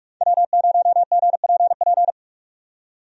Morse code, heard as a keyed tone, about 46 wpm, 700 Hz. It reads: W0GPP
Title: USA Call Signs - 5 Characters - 45wpm